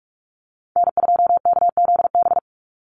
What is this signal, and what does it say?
Morse code, heard as a keyed tone, about 44 wpm, 700 Hz. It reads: N2KZB
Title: USA Call Signs 45wpm